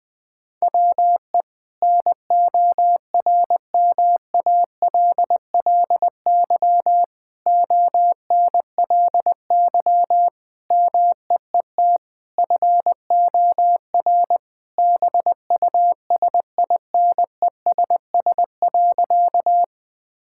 Morse code, heard as a keyed tone, about 21 wpm, 700 Hz. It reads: WE NORMALLY ONLY MEET FOR BUSINESS.
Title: Sentences from Top 1000 Words - Review Part 6 - 20wpm